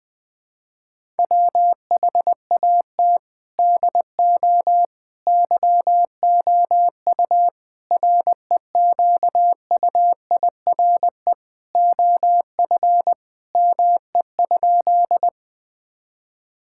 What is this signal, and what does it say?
Morse code, heard as a keyed tone, about 20 wpm, 700 Hz. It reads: WHAT DO YOU REQUIRE OF ME?
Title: Sentences from Top 1000 Words - Review Part 6 - 20wpm